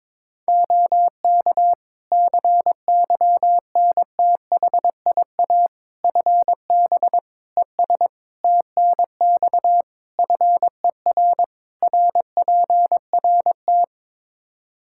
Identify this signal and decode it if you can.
Morse code, heard as a keyed tone, about 22 wpm, 700 Hz. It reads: OK CYNTHIA FB ES TNX FER RPRT
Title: K1USN SST QSO Practice 22wpm